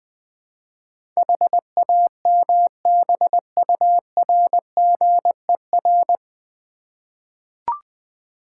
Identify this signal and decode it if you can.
Morse code, heard as a keyed tone, about 20 wpm, 700 Hz. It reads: HAMBURGER
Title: Easy Words with Hints - 20wpm